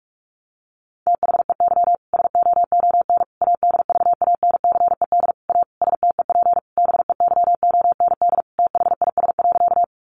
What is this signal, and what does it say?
Morse code, heard as a keyed tone, about 44 wpm, 700 Hz. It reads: THEY SOON ADVANCED A STEP BEYOND THIS.
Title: Sentences from Top 1200 Words 45wpm